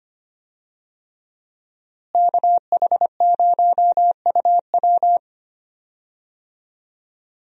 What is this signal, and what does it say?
Morse code, heard as a keyed tone, about 25 wpm, 700 Hz. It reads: KH0UW
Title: USA Call Signs - 5 Characters - Repeated 3x - 25wpm